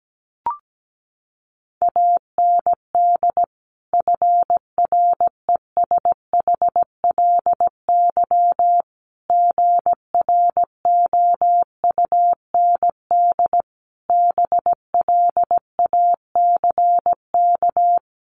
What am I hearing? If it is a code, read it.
Morse code, heard as a keyed tone, about 17 wpm, 700 Hz. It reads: AND FRESHLY GROUND BLACK
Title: Sets of 5 Words Encore 17wpm